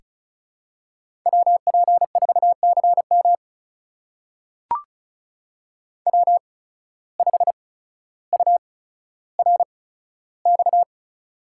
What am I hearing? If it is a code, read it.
Morse code, heard as a keyed tone, about 36 wpm, spaced as for 15 wpm, 700 Hz. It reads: WP4CM W5URX
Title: USA Call Signs - 8x Character Spacing - 35wpm